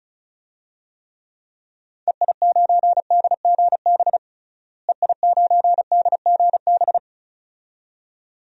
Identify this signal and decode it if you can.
Morse code, heard as a keyed tone, about 36 wpm, 700 Hz. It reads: EI9DGB EI9DGB
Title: DXCC-100 Call Signs - 6 Characters - Repeated 2x - 35wpm